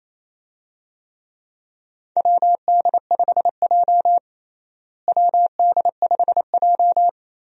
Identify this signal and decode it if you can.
Morse code, heard as a keyed tone, about 28 wpm, 700 Hz. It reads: WD5J WD5J
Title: USA Call Signs - 4 Characters - Repeated 2x - 28wpm